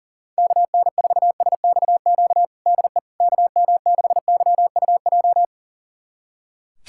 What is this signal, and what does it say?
Morse code, heard as a keyed tone, about 40 wpm, 700 Hz. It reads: KN4SXQ DE KM6YUJ
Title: USA Callsign DE Callsign 40wpm